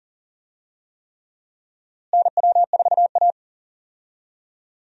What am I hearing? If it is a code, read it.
Morse code, heard as a keyed tone, about 40 wpm, 700 Hz. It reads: NW4A
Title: USA Call Signs - 4 Characters - Repeated 3x - 40wpm